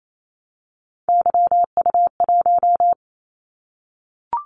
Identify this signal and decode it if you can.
Morse code, heard as a keyed tone, about 28 wpm, 700 Hz. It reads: YU1
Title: Callsign Prefixes for Top 25 DXCC - 28wpm